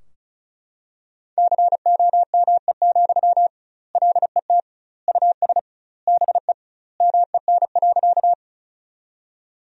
Morse code, heard as a keyed tone, about 36 wpm, 700 Hz. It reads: COME, LET US BE MEN.
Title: 2x Word-Spacing - Sentences from Top 300 Words - 35wpm